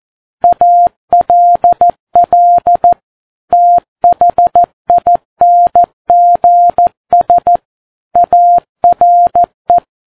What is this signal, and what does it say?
Morse code, heard as a keyed tone, about 14 wpm, 700 Hz. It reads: ALL THINGS ARE
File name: Ham Radio Sat